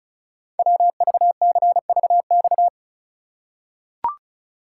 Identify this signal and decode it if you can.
Morse code, heard as a keyed tone, about 36 wpm, 700 Hz. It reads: WVCVX
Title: Code Groups - 5 Letter:Numbers - 35wpm